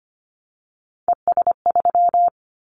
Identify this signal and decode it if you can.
Morse code, heard as a keyed tone, about 25 wpm, 700 Hz. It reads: ES3
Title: Callsign Prefixes for Top 50 DXCC - 25wpm